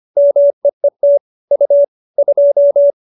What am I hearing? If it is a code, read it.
Morse code, heard as a keyed tone, about 25 wpm, 570 Hz. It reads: MEET U 2